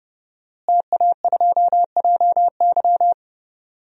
Morse code, heard as a keyed tone, about 30 wpm, 700 Hz. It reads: TA2JY
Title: DXCC-100 Call Signs - 5 Characters - Repeated 2x - 30wpm